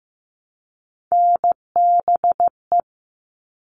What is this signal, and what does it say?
Morse code, heard as a keyed tone, about 15 wpm, 700 Hz. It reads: NBE
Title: All Cut Numbers - 3-Digit 15wpm